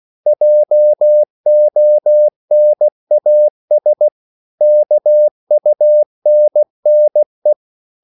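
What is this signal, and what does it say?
Morse code, heard as a keyed tone, about 16 wpm, 600 Hz. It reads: JONAS KUNNE
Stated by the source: Slow (QRS) Morse code at 16 wpm, from CW med Gnister.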